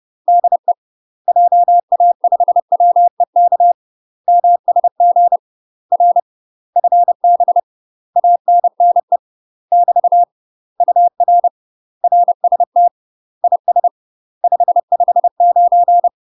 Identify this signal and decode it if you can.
Morse code, heard as a keyed tone, about 30 wpm, 700 Hz. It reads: DE JA5WEK MSG R FB ANNE <BT> UR RST IS 559
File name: UK MASTER QSOs TRAINING / 30WPM / UK MASTER QSOs 0041